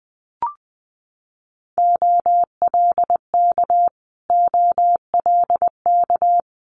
Morse code, heard as a keyed tone, about 19 wpm, 700 Hz. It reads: OLK OLK
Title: Multi-letter Phonemes Repeated 3x 20wpm